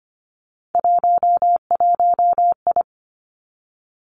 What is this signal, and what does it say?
Morse code, heard as a keyed tone, about 25 wpm, 700 Hz. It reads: 11I
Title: Three Challenging Letter-Number Combinations 25wpm